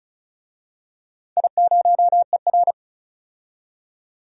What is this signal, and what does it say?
Morse code, heard as a keyed tone, about 36 wpm, 690 Hz. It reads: I0ER
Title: DXCC-100 Call Signs - 4 Characters - 35wpm